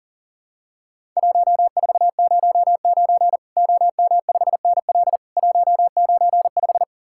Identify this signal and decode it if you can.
Morse code, heard as a keyed tone, about 40 wpm, 700 Hz. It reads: 1409 OM5NL 195
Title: Worked All Europe DX Contest - QTC Exchange - 40wpm